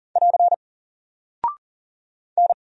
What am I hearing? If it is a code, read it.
Morse code, heard as a keyed tone, about 40 wpm, 700 Hz. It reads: <AR> N
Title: Single Character (Letter:Number:Punctuation:Prosign) Alt 40wpm